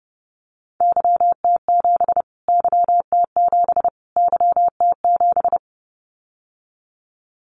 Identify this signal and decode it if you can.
Morse code, heard as a keyed tone, about 30 wpm, 700 Hz. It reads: YT7 YT7 YT7
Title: Callsign Prefixes for Top 25 DXCC - Repeated 3x - 30wpm